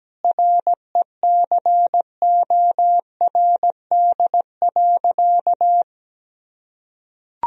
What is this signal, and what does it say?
Morse code, heard as a keyed tone, about 17 wpm, 700 Hz. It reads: RECORD.
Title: Sentences from Top 1000 Words - Review Part 3 - 17wpm